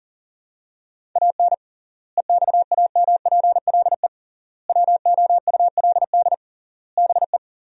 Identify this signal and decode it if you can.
Morse code, heard as a keyed tone, about 40 wpm, 700 Hz. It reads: AN EXAMPLE WOULD BE
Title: 3x Word-Spacing - Sentences from Top 400 Words - 40wpm